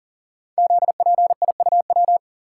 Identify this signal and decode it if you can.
Morse code, heard as a keyed tone, about 40 wpm, 700 Hz. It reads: ZPIUW
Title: Code Groups - 5 Letters - 40wpm